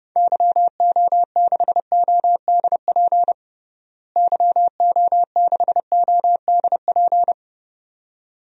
Morse code, heard as a keyed tone, about 32 wpm, 700 Hz. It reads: YO6ODP YO6ODP
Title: DXCC-100 Call Signs - 6 Characters - Repeated 2x - 30wpm